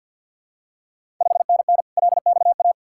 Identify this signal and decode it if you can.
Morse code, heard as a keyed tone, about 53 wpm, 690 Hz. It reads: 5NN LXA
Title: Ontario QSO Party Exchange 50wpm